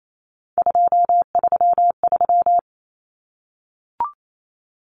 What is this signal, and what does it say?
Morse code, heard as a keyed tone, about 28 wpm, 700 Hz. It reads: 233
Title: Numbers - 3-Digit 28wpm